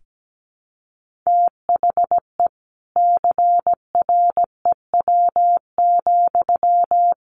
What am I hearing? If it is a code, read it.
Morse code, heard as a keyed tone, about 17 wpm, 700 Hz. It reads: THE CREW,
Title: Sentences from Top 1500 Words 17wpm